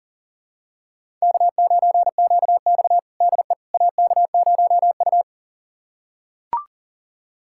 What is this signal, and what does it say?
Morse code, heard as a keyed tone, about 40 wpm, 700 Hz. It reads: K9QX DE AK0U
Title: USA Callsign DE Callsign - 4 Characters - 40wpm